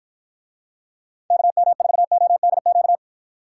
Morse code, heard as a keyed tone, about 50 wpm, 700 Hz. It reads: KM4ODQ